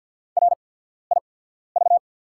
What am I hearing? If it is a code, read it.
Morse code, heard as a keyed tone, about 48 wpm, spaced as for 14 wpm, 690 Hz. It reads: RIV
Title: 8x Character Spacing - Top 300 Words - 50wpm